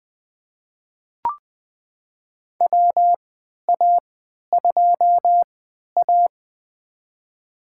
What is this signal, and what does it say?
Morse code, heard as a keyed tone, about 21 wpm, spaced as for 12 wpm, 700 Hz. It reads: WA2A
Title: USA Call Signs - 3x Character Spacing - 20wpm